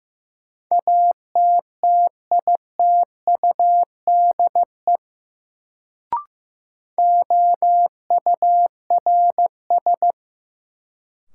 Morse code, heard as a keyed tone, about 15 wpm, 700 Hz. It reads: ATTITUDE OURS
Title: Sentences from Top 1300 Words - New Word Drill Alt - 15wpm